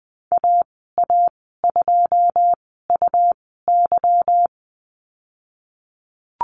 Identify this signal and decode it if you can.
Morse code, heard as a keyed tone, about 20 wpm, spaced as for 15 wpm, 700 Hz. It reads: AA2UY
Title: 179 Lesson 29g - Slash - Call Signs 20wpm